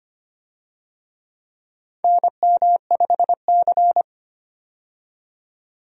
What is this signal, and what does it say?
Morse code, heard as a keyed tone, about 25 wpm, 700 Hz. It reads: NM5C
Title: USA Call Signs - 4 Characters - 25wpm